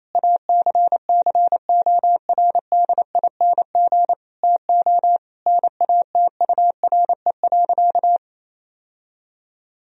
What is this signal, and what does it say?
Morse code, heard as a keyed tone, about 28 wpm, 700 Hz. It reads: ACCORDING TO NATURE.
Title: Sentences from Top 1000 Words - Review Part 4 - 28wpm